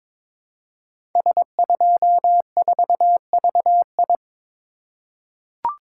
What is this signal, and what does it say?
Morse code, heard as a keyed tone, about 22 wpm, 700 Hz. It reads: S24VI
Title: Code Groups - 5 Letter:Numbers - 22wpm